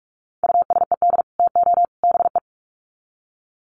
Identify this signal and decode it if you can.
Morse code, heard as a keyed tone, about 44 wpm, 700 Hz. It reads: USED TO BE
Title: Sets of 3 Words 45wpm